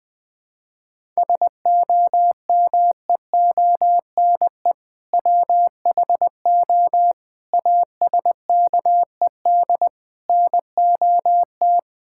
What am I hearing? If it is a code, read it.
Morse code, heard as a keyed tone, about 21 wpm, 700 Hz. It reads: SOMEONE WHO ASKED NOT
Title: Sentences with 7-Word Phrases - 20wpm